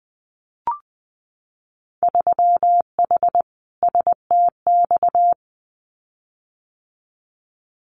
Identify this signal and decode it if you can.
Morse code, heard as a keyed tone, about 20 wpm, 700 Hz. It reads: 3H STX
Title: Winter Field Day Exchanges 20wpm